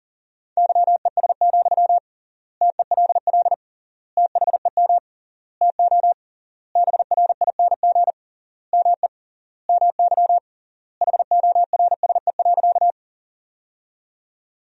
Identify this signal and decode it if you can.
Morse code, heard as a keyed tone, about 40 wpm, 700 Hz. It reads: YES, TELL THEM TO BRING ME MY HORSE.
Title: 3x Word-Spacing - Sentences from Top 500 Words - 40wpm